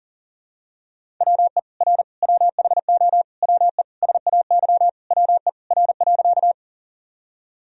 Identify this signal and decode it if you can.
Morse code, heard as a keyed tone, about 40 wpm, 700 Hz. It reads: WE R WHO WE SAY WE R.
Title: Sentences from Top 200 Words and Abbreviations 40wpm